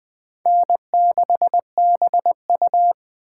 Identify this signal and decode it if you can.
Morse code, heard as a keyed tone, about 20 wpm, 700 Hz. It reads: N6BU
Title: Speed Racing - USA 1x2 Call Signs 17wpm